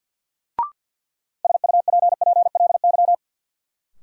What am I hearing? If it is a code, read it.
Morse code, heard as a keyed tone, about 53 wpm, 700 Hz. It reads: SUPPLY
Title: Sentences from Top 1200 Words - New Word Drill Alt - 50wpm